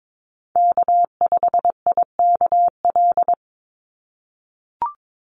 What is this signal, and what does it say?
Morse code, heard as a keyed tone, about 22 wpm, 700 Hz. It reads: K5IKL
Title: USA Call Signs 22wpm